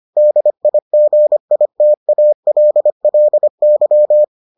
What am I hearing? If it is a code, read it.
Morse code, heard as a keyed tone, about 25 wpm, 600 Hz. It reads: DIGITALLY